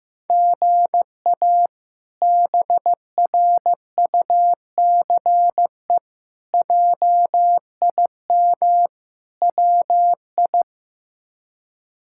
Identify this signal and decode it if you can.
Morse code, heard as a keyed tone, about 16 wpm, 700 Hz. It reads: GA BRUCE JIM WI
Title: K1USN SST QSO Practice - 15wpm